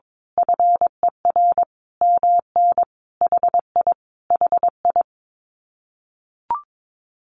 Morse code, heard as a keyed tone, about 22 wpm, 700 Hz. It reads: FER MN HI HI
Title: Common QSO Phrases Encore 22wpm